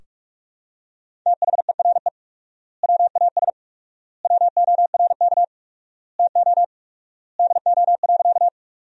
Morse code, heard as a keyed tone, about 46 wpm, 700 Hz. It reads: THERE WAS WORK TO DO.
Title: 4x Word-Spacing - Sentences from Top 200 Words - 45wpm